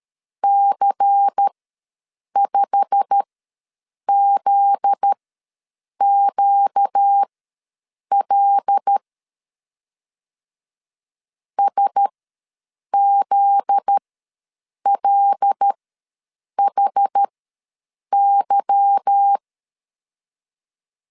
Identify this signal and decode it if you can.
Morse code, heard as a keyed tone, about 13 wpm, spaced as for 7 wpm, 800 Hz. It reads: C5ZQL SZLHY
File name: LEZIONI / 8 YJ45 / 8 YJ45 ws 12wpm 3146